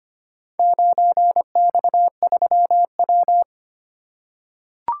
Morse code, heard as a keyed tone, about 25 wpm, 700 Hz. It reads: 9X3W
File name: practice / 25wpm / Four Letter-Number Combinations - 25wpm